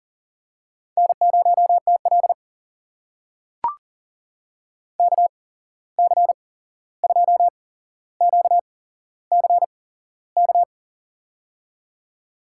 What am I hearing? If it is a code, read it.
Morse code, heard as a keyed tone, about 40 wpm, spaced as for 15 wpm, 700 Hz. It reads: N0TL KC2QCK
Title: USA Call Signs - 8x Character Spacing - 40wpm